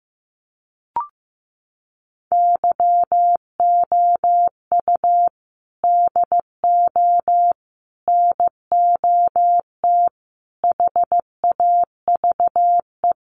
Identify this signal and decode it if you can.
Morse code, heard as a keyed tone, about 15 wpm, 700 Hz. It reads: YOU DO NOT HAVE
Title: Sets of 5 Words Encore 15wpm